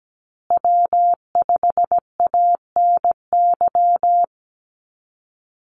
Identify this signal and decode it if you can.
Morse code, heard as a keyed tone, about 17 wpm, 700 Hz. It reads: W5ANY
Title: USA Call Signs Repeated 3x 17wpm